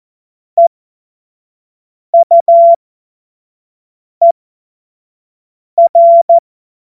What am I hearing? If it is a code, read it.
Morse code, heard as a keyed tone, about 14 wpm, spaced as for 5 wpm, 680 Hz. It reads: EUER
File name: Morsekurs / Witze / Witz 09. Februar 2021, 25 F70